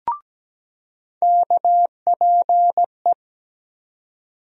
Morse code, heard as a keyed tone, about 17 wpm, 700 Hz. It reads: KPE
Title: USA Call Sign Postfixes - 3 Letters 17wpm